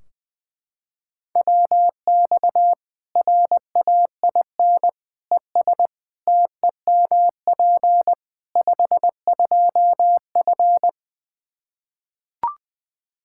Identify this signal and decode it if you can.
Morse code, heard as a keyed tone, about 20 wpm, 700 Hz. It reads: WX RAIN ES TEMP 52F
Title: Practice QSO - Exchange 2 - Info - 20wpm